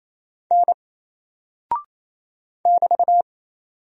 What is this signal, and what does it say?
Morse code, heard as a keyed tone, about 27 wpm, 700 Hz. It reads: N <BT>
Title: Single Character (Letter:Number:Punctuation:Prosign) 28wpm